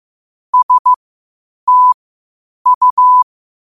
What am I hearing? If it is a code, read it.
Morse code, heard as a keyed tone, about 15 wpm, spaced as for 8 wpm, 1000 Hz. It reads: STU
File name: qotdfiles / 8wpm